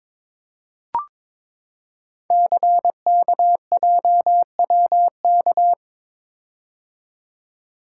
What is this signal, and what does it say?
Morse code, heard as a keyed tone, about 22 wpm, 700 Hz. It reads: CKJWK
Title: Code Groups - 5 Letters - 22wpm